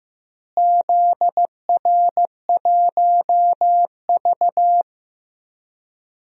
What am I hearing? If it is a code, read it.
Morse code, heard as a keyed tone, about 15 wpm, 700 Hz. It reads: ZR1V
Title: DXCC-100 Call Signs - 4 Characters - Repeated 3x - 15wpm